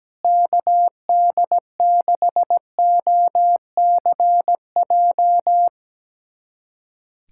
Morse code, heard as a keyed tone, about 17 wpm, 700 Hz. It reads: KD6OCJ
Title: USA Callsign DE Callsign - 6 Characters - 17wpm